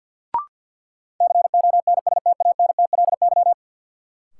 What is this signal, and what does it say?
Morse code, heard as a keyed tone, about 52 wpm, 700 Hz. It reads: CONSTANTLY
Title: Sentences from Top 1000 Words - Review Part 7 - New Word Drill - 50wpm